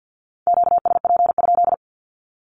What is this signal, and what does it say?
Morse code, heard as a keyed tone, about 48 wpm, 700 Hz. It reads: QSP?
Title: Q-Codes 50wpm